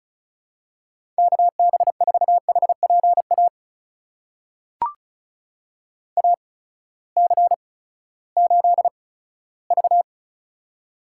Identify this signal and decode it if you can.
Morse code, heard as a keyed tone, about 34 wpm, spaced as for 17 wpm, 700 Hz. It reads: KB4HPA AC8V
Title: USA Call Signs - 8x Character Spacing - 35wpm